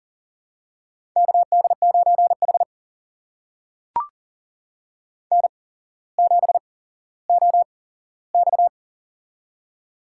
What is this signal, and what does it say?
Morse code, heard as a keyed tone, about 40 wpm, spaced as for 17 wpm, 700 Hz. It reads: KD9H N7OX
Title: USA Call Signs - 4 Characters - 8x Character Spacing - 40wpm